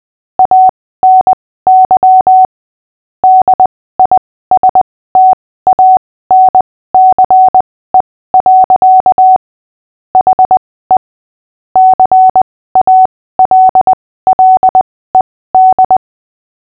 Morse code, heard as a keyed tone, about 20 wpm, spaced as for 15 wpm, 750 Hz. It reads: ANY DISTANCE. HE CALLED
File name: cw / fists / Short-Story-6-15wpm